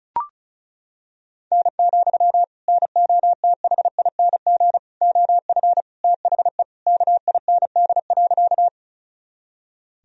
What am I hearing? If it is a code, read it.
Morse code, heard as a keyed tone, about 34 wpm, 700 Hz. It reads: N, NOTHING OF THE KIND.
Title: Sentences from Top 500 Words and Abbreviations 35wpm